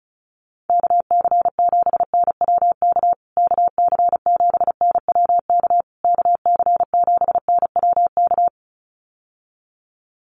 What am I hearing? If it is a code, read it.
Morse code, heard as a keyed tone, about 34 wpm, 700 Hz. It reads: KC7NWK KC7NWK KC7NWK
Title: USA Call Signs Repeated 3x 35wpm